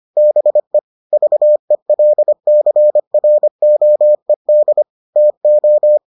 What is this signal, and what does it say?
Morse code, heard as a keyed tone, about 25 wpm, 600 Hz. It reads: BE VELCROED TO